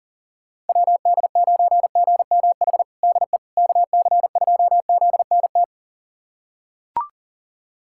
Morse code, heard as a keyed tone, about 40 wpm, 700 Hz. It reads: WD9GMH DE KC2ZNT
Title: USA Callsign DE Callsign 40wpm